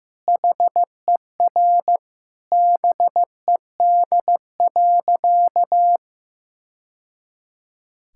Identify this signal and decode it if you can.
Morse code, heard as a keyed tone, about 15 wpm, 700 Hz. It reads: HER BED.
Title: Sentences from Top 1000 Words - Review Part 6 - 15wpm